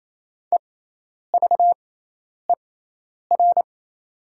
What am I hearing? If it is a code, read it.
Morse code, heard as a keyed tone, about 28 wpm, spaced as for 10 wpm, 700 Hz. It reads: EVER
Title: 6x Character Spacing - Top 500 Words - 28wpm